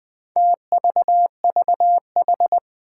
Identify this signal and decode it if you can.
Morse code, heard as a keyed tone, about 20 wpm, 700 Hz. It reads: TVVH
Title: Four Challenging Letter-Number Combinations 20wpm